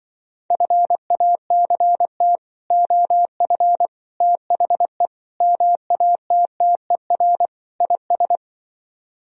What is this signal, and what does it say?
Morse code, heard as a keyed tone, about 24 wpm, 700 Hz. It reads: FACT OF THE MATTER IS
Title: Speed-Racing - Sets of 5 Words 20wpm